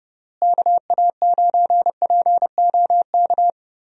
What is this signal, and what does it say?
Morse code, heard as a keyed tone, about 30 wpm, 700 Hz. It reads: KA9POK
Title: USA Call Signs - 6 Characters - Repeated 3x - 30wpm